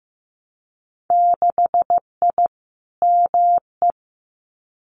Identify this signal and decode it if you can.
Morse code, heard as a keyed tone, about 15 wpm, 700 Hz. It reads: 6I ME
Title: Winter Field Day Exchanges Encore 15wpm